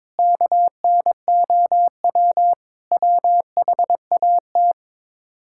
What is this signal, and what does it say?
Morse code, heard as a keyed tone, about 22 wpm, 700 Hz. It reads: KNOW WHAT
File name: practice / 22wpm / Build A Phrase - 5-Word Phrases - 22wpm